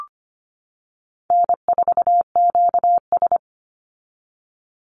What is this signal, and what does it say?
Morse code, heard as a keyed tone, about 25 wpm, 700 Hz. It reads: N4QS
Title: Easy Call Signs - 25wpm